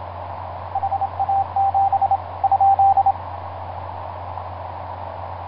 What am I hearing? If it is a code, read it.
Morse code, heard as a keyed tone, about 27 wpm, 780 Hz, with radio noise. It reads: HA7 ?
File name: vk9x-g6ay ha7lw etc 10m cw 23feb2010 0836